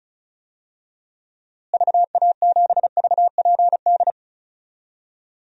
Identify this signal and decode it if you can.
Morse code, heard as a keyed tone, about 36 wpm, 700 Hz. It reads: VA7VPD
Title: DXCC-100 Call Signs - 6 Characters - Repeated 3x - 35wpm